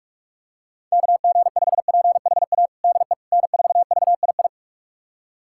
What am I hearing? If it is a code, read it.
Morse code, heard as a keyed tone, about 47 wpm, 700 Hz. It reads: KG5PHA DE N4VII
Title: USA Callsign DE Callsign 45wpm